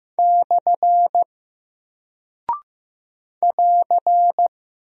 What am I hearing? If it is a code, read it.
Morse code, heard as a keyed tone, about 15 wpm, 700 Hz. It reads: / <AR>
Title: Single Character (Letter:Number:Punctuation:Prosign) 15wpm